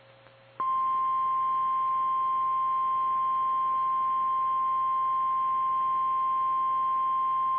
Emergency Alert System (SAME) tones (the 1050 Hz attention tone). It is heard from 1.2 to 6.9 s.
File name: eas / KKPRKXPN / KKPRKXPN